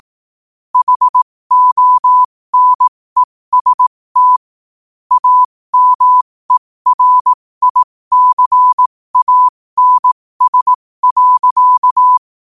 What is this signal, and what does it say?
Morse code, heard as a keyed tone, about 18 wpm, spaced as for 15 wpm, 1000 Hz. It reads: HONEST AMERICANS.
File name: qotdfiles / 15wpm